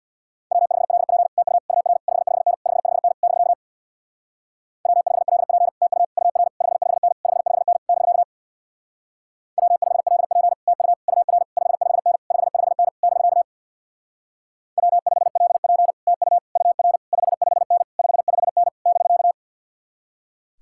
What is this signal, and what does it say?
Morse code, heard as a keyed tone, about 66 wpm, 700 Hz. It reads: W5LP TU UR 55N 55N <BK> W5LP TU UR 55N 55N <BK> W5LP TU UR 55N 55N <BK> W5LP TU UR 55N 55N <BK>